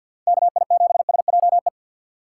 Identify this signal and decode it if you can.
Morse code, heard as a keyed tone, about 50 wpm, 700 Hz. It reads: KI7SJE
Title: USA Call Signs - 6 Characters - 3x Character Spacing - 50wpm